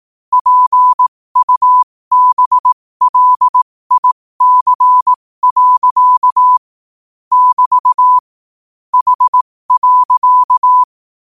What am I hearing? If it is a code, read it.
Morse code, heard as a keyed tone, about 18 wpm, spaced as for 15 wpm, 1000 Hz. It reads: PUBLIC. <BT> H.
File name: qotdfiles / 15wpm